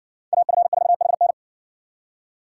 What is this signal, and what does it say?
Morse code, heard as a keyed tone, about 58 wpm, 700 Hz. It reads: AF4HN